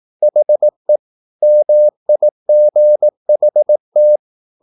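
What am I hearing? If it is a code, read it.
Morse code, heard as a keyed tone, about 18 wpm, 600 Hz. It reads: HE MIGHT
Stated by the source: Morse practice at 18 wpm, English text, from CW med Gnister.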